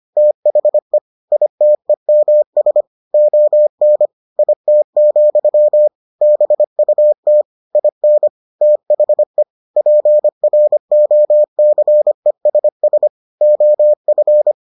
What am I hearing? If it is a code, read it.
Morse code, heard as a keyed tone, about 25 wpm, 600 Hz. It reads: THE ITEMS ON IT, BUT IN THE PROCESS OF